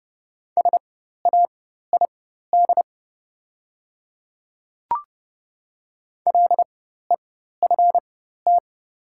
Morse code, heard as a keyed tone, about 29 wpm, spaced as for 14 wpm, 700 Hz. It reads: SAID LEFT
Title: 4x Character Spacing - Top 300 Words - 30wpm